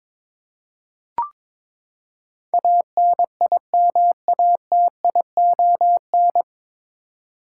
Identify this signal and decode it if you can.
Morse code, heard as a keyed tone, about 22 wpm, 700 Hz. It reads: ANIMATION
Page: Morse Code Ninja practice 22wpm